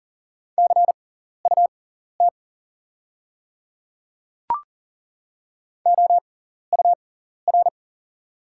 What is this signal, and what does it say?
Morse code, heard as a keyed tone, about 40 wpm, spaced as for 14 wpm, 690 Hz. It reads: CUT OUR
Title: 6x Character Spacing - Top 500 Words - 40wpm